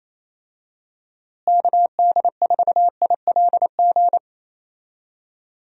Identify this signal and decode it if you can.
Morse code, heard as a keyed tone, about 28 wpm, 700 Hz. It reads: KD4ILG
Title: USA Call Signs - 6 Characters - Repeated 2x - 28wpm